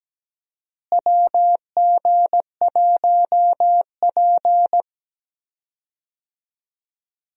Morse code, heard as a keyed tone, about 17 wpm, 700 Hz. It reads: WG1P
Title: USA Call Signs - 4 Characters - Repeated 2x - 17wpm